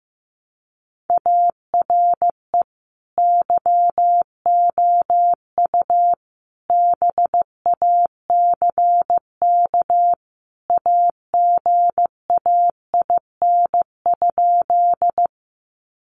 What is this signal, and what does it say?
Morse code, heard as a keyed tone, about 15 wpm, 700 Hz. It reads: ARE YOU BACK AGAIN?
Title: Sentences from Top 200 Words 15wpm